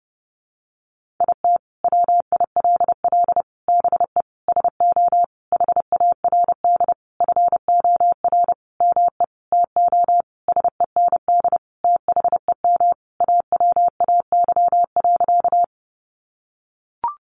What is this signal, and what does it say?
Morse code, heard as a keyed tone, about 30 wpm, 700 Hz. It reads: IT WILL BE SO HARD FOR ME TO SEND THEM AWAY.
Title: Sentences from Top 1500 Words 30wpm